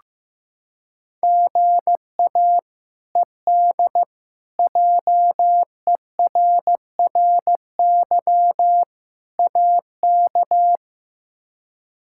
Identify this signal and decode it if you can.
Morse code, heard as a keyed tone, about 15 wpm, 700 Hz. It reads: GA ED JERRY AK
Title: K1USN SST QSO Practice - 15wpm